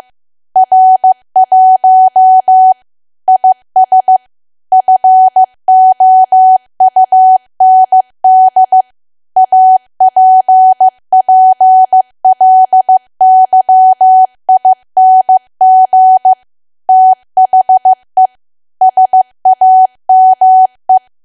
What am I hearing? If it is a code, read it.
Morse code, heard as a keyed tone, about 15 wpm, 750 Hz. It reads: R1 IS FOUND APPLYING THE SAME
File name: cwa / W1AW-Level2 / 15WPM